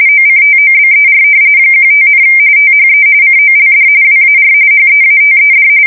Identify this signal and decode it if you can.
RTTY, about 45.45 baud. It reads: OCTOBER 15, THE COLD WAR HEATED
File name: fsk170 / battleshipnj